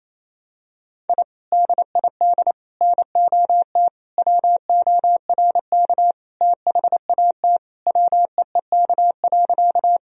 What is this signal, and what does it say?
Morse code, heard as a keyed tone, about 28 wpm, 700 Hz. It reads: I DID NOT WORK THAT WEEK.
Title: Sentences from Top 1000 Words - Review Part 2 - 28wpm